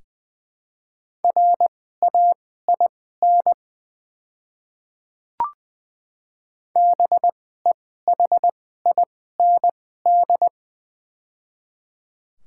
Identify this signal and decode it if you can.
Morse code, heard as a keyed tone, about 20 wpm, spaced as for 14 wpm, 700 Hz. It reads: RAIN BEHIND
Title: 2x Character Spacing - Top 500 Words - 20wpm